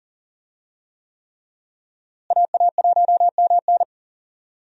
Morse code, heard as a keyed tone, about 40 wpm, 700 Hz. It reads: AA1MN
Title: USA Call Signs - 5 Characters - Repeated 3x - 40wpm